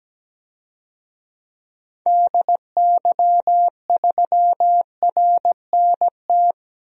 Morse code, heard as a keyed tone, about 17 wpm, 700 Hz. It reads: DY3RNT
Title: DXCC-100 Call Signs - 6 Characters - 17wpm